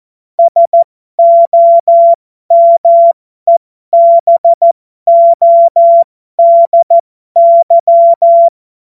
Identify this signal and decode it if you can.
Morse code, heard as a keyed tone, about 14 wpm, spaced as for 12 wpm, 680 Hz. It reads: SOMEBODY
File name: Morsekurs / Witze